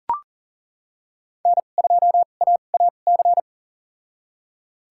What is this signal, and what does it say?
Morse code, heard as a keyed tone, about 38 wpm, spaced as for 28 wpm, 700 Hz. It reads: N2AAC
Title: USA Call Signs - 5 Characters - 2x Character Spacing - 40wpm